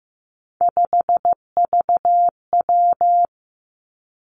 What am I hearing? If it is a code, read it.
Morse code, heard as a keyed tone, about 15 wpm, 700 Hz. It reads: 5VW
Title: Three Challenging Letter-Number Combinations 15wpm